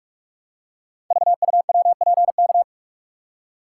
Morse code, heard as a keyed tone, about 44 wpm, 690 Hz. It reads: VUWPK
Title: Code Groups - 5 Letters - 45wpm